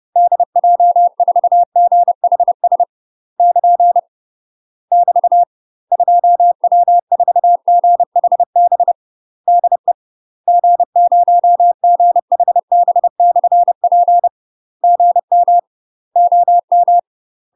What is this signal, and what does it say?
Morse code, heard as a keyed tone, about 30 wpm, 700 Hz. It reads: DJ4GHS <KN> <BT> 2W4GHB DE G0GHB/P GM OM
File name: UK MASTER QSOs TRAINING / 30WPM / UK MASTER QSOs 0038